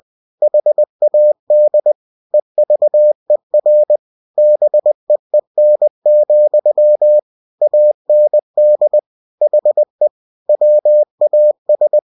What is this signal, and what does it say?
Morse code, heard as a keyed tone, about 20 wpm, 600 Hz. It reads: HAD EVER BEEN, AND HE WAS
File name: cw-books / winnie-the-pooh / winnie-20wpm-page-0106